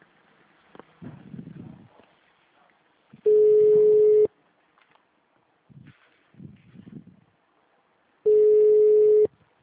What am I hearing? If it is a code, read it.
Ringback tone (Europe, 425 Hz).